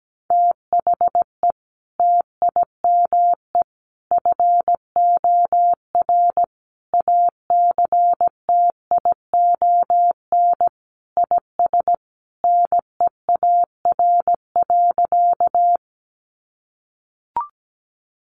Morse code, heard as a keyed tone, about 17 wpm, 700 Hz. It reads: THE TIME FOR ACTION IS NEAR.
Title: Sentences from Top 700 Words 17wpm